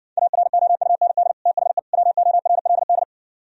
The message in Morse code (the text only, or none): AROUND THE WORLD